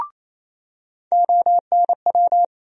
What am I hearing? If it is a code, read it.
Morse code, heard as a keyed tone, about 28 wpm, 700 Hz. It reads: ONW
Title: USA Call Sign Postfixes - 3 Letters 28wpm